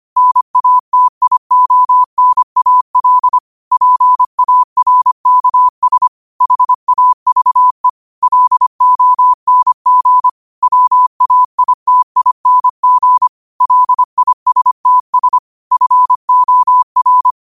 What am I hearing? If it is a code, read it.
Morse code, heard as a keyed tone, about 25 wpm, 1000 Hz. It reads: NATIONAL PARKS HAVE LONG WAITING LISTS FOR